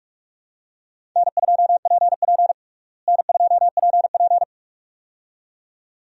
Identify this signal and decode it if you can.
Morse code, heard as a keyed tone, about 42 wpm, 700 Hz. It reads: N2PP N2PP
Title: USA Call Signs - 4 Characters - Repeated 2x - 45wpm